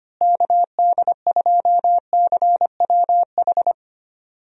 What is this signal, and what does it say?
Morse code, heard as a keyed tone, about 25 wpm, 700 Hz. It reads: KD2CWH
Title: USA Call Signs - 6 Characters - Repeated 3x - 25wpm